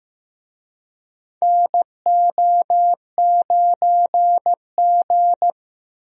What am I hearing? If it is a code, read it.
Morse code, heard as a keyed tone, about 15 wpm, 700 Hz. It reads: NO9G
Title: USA Call Signs - 4 Characters - 15wpm